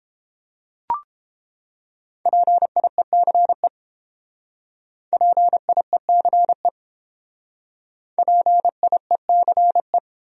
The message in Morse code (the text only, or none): PIECE PIECE PIECE